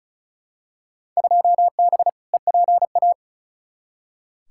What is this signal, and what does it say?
Morse code, heard as a keyed tone, about 36 wpm, 700 Hz. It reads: 2B EPA
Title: Field Day Exchanges Encore - 35wpm